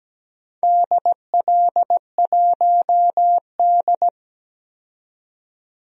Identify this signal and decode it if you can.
Morse code, heard as a keyed tone, about 17 wpm, 700 Hz. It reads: DL1D
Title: DXCC-100 Call Signs - 4 Characters - 17wpm